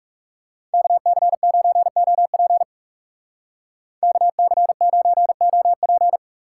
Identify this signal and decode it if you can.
Morse code, heard as a keyed tone, about 40 wpm, 700 Hz. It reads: KC9OP KC9OP